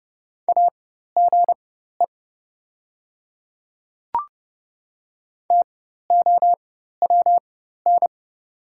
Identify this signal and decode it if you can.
Morse code, heard as a keyed tone, about 31 wpm, spaced as for 14 wpm, 700 Hz. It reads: AGE TOWN
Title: 4x Character Spacing - Top 500 Words - 30wpm